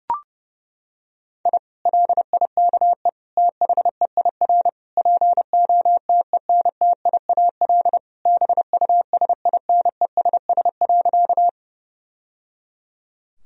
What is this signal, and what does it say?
Morse code, heard as a keyed tone, about 32 wpm, 700 Hz. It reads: I LIKE THEIR POTENTIAL BUSINESS.
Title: Sentences from Top 1000 Words - Review Part 4 - 30wpm